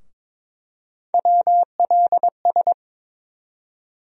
Morse code, heard as a keyed tone, about 22 wpm, 700 Hz. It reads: WLS
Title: USA Call Sign Postfixes - 3 Letters 22wpm